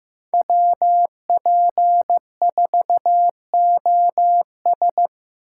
Morse code, heard as a keyed tone, about 15 wpm, 700 Hz. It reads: WP4OS
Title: USA Call Signs - 5 Characters - 4x Character Spacing - 15wpm